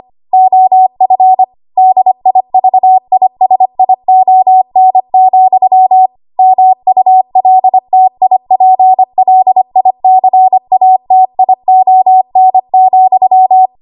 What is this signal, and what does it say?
Morse code, heard as a keyed tone, about 25 wpm, 750 Hz. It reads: OF DIVISION, MULTIPLICATION,
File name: Archive / 25 WPM / 25WPM